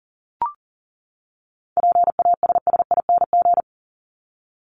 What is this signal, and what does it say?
Morse code, heard as a keyed tone, about 40 wpm, 700 Hz. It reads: PASSING
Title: ING Suffix - Top 100 Words with Base Words 40wpm